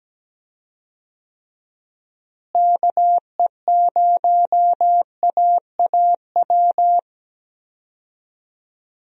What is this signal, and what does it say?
Morse code, heard as a keyed tone, about 17 wpm, 700 Hz. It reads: KE0AAW
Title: USA Call Signs - 6 Characters - Repeated 2x - 17wpm